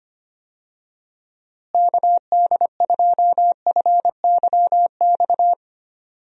Morse code, heard as a keyed tone, about 25 wpm, 700 Hz. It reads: KD2FYX
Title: USA Call Signs - 6 Characters - Repeated 2x - 25wpm